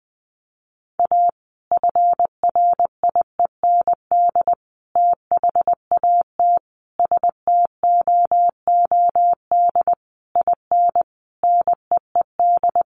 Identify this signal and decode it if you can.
Morse code, heard as a keyed tone, about 20 wpm, 700 Hz. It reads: A FRIEND THAT STOOD IN NEED
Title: Sentences from Top 1500 Words 20wpm